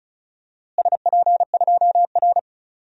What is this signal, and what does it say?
Morse code, heard as a keyed tone, about 34 wpm, 700 Hz. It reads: SP2R
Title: DXCC-100 Call Signs - 4 Characters - Repeated 2x - 35wpm